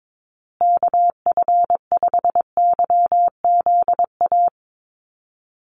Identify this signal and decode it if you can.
Morse code, heard as a keyed tone, about 22 wpm, 700 Hz. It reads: KF5YZA
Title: USA Call Signs Repeated 3x 22wpm